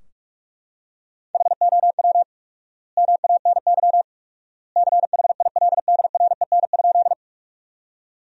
Morse code, heard as a keyed tone, about 41 wpm, 700 Hz. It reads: HOW MANY CHILDREN?
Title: 4x Word-Spacing - Sentences from Top 300 Words - 45wpm